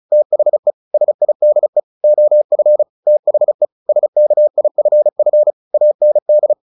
Morse code, heard as a keyed tone, about 34 wpm, 600 Hz. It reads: THE SIDE OF THE SKIFF AND
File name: the old man and the sea / the old man and the sea 1 35wpm